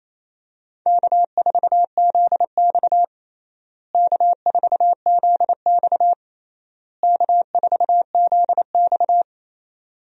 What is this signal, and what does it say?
Morse code, heard as a keyed tone, about 27 wpm, 700 Hz. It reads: K4ZX K4ZX K4ZX
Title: USA Call Signs - 4 Characters - Repeated 3x - 28wpm